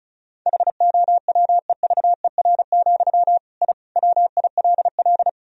Morse code, heard as a keyed tone, about 36 wpm, 700 Hz. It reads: HOWEVER, I WILL
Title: Sentences from Top 1000 Words - Review Part 1 - 35wpm